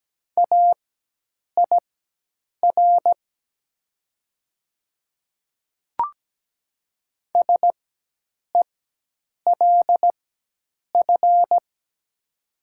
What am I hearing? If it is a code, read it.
Morse code, heard as a keyed tone, about 17 wpm, spaced as for 8 wpm, 700 Hz. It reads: AIR SELF
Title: 4x Character Spacing - Top 300 Words - 17wpm